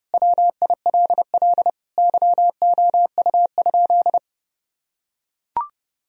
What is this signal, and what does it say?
Morse code, heard as a keyed tone, about 32 wpm, 700 Hz. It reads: WILL YOU?
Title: Sentences from Top 1000 Words - Review Part 4 - 30wpm